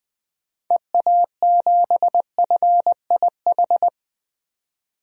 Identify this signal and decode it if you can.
Morse code, heard as a keyed tone, about 20 wpm, 700 Hz. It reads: EA7FIH
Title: DXCC-100 Call Signs - 6 Characters - 20wpm